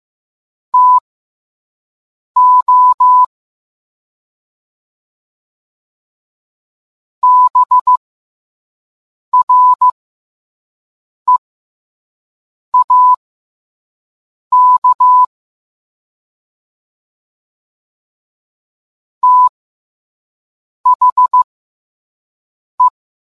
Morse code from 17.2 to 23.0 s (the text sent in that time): THE